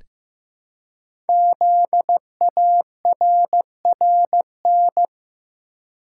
Morse code, heard as a keyed tone, about 15 wpm, 700 Hz. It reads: ZARRN
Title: Code Groups - 5 Letters - 15wpm